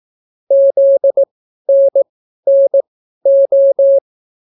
Morse code, heard as a keyed tone, about 18 wpm, spaced as for 12 wpm, 550 Hz. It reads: ZNNO